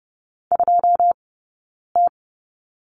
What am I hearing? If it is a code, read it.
Morse code, heard as a keyed tone, about 30 wpm, spaced as for 9 wpm, 690 Hz. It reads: 2T
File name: Lesson 17d - Letter u - Two Characters 30wpm